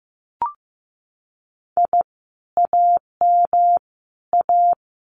Morse code, heard as a keyed tone, about 15 wpm, 700 Hz. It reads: I AM A
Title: Common QSO Phrases Encore 15wpm